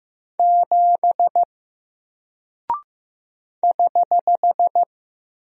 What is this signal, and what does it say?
Morse code, heard as a keyed tone, about 15 wpm, 700 Hz. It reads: 7 <HH>
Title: Single Character (Letter:Number:Punctuation:Prosign) Alt 15wpm